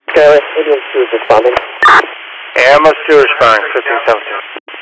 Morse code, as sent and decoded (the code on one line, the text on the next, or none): none